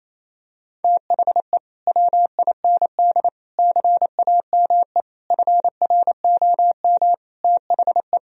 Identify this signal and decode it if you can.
Morse code, heard as a keyed tone, about 28 wpm, 700 Hz. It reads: THE WIND CAME FROM THE
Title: Sentences from Top 1000 Words - Review Part 2 - 28wpm